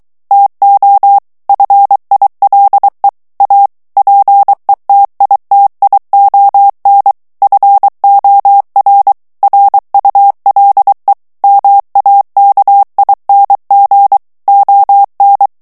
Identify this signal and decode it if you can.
Morse code, heard as a keyed tone, about 23 wpm, 800 Hz. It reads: TO FILE A PETITION FOR RULE MAKING ON
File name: ANGLAIS texte clair 3 vitesse 22 mots minute